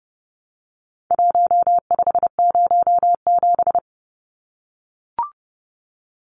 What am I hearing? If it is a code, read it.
Morse code, heard as a keyed tone, about 30 wpm, 700 Hz. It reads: 1507
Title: Numbers - Contest Style - 4-Digit 30wpm